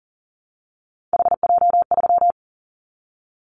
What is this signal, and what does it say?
Morse code, heard as a keyed tone, about 41 wpm, 700 Hz. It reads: HJ3
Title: Callsign Prefixes for Top 100 DXCC - 40wpm